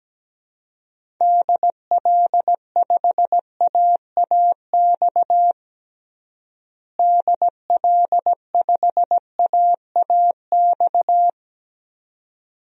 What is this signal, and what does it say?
Morse code, heard as a keyed tone, about 17 wpm, 700 Hz. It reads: DL5AAX DL5AAX
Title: DXCC-100 Call Signs - 6 Characters - Repeated 2x - 17wpm